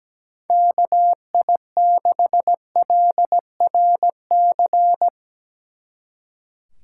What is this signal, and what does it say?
Morse code, heard as a keyed tone, about 17 wpm, 700 Hz. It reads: KI6LRC
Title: Speed Racing - USA Call Signs 17wpm